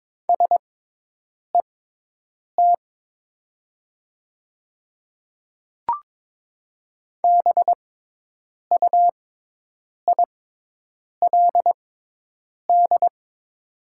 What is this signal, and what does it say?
Morse code, heard as a keyed tone, about 22 wpm, spaced as for 8 wpm, 700 Hz. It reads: SET BUILD
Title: 6x Character Spacing - Top 200 Words - 22wpm